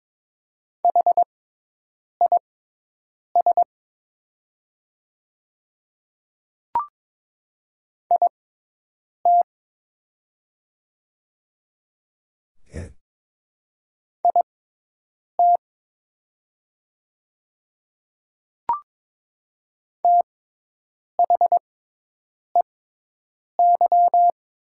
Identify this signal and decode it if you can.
Morse code, heard as a keyed tone, about 22 wpm, spaced as for 8 wpm, 700 Hz. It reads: HIS IT IT THEY
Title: 6x Character Spacing - Top 200 Words - 22wpm